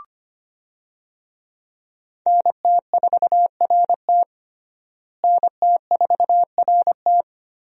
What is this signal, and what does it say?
Morse code, heard as a keyed tone, about 25 wpm, 700 Hz. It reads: NT4RT NT4RT
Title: USA Call Signs - 5 Characters - Repeated 3x - 25wpm